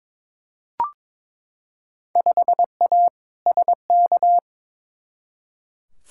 Morse code, heard as a keyed tone, about 22 wpm, 700 Hz. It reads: 5A SK
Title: Field Day Exchanges Encore - 22wpm